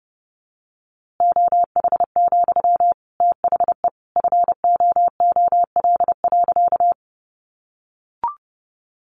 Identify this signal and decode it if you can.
Morse code, heard as a keyed tone, about 30 wpm, 700 Hz. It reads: OH, THE FOOL.
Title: A Sound Of Thunder 30wpm